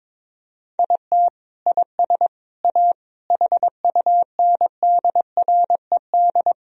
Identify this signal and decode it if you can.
Morse code, heard as a keyed tone, about 22 wpm, 700 Hz. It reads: IT IS A HUNDRED
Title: Sentences from Top 1000 Words - Review Part 2 - 22wpm